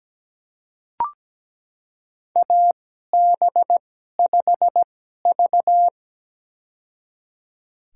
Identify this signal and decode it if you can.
Morse code, heard as a keyed tone, about 17 wpm, spaced as for 12 wpm, 700 Hz. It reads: AB5V
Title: USA Call Signs - 4 Characters - 2x Character Spacing - 17wpm